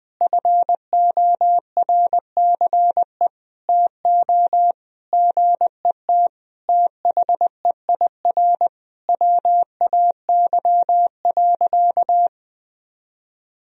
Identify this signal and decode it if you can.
Morse code, heard as a keyed tone, about 20 wpm, 700 Hz. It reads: FORCE TO GET THEIR WAY.
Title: Sentences from Top 1000 Words - Review Part 4 - 20wpm